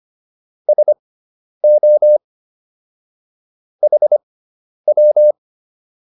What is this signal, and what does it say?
Morse code, heard as a keyed tone, about 25 wpm, spaced as for 10 wpm, 600 Hz. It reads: SO HW
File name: QSO207 10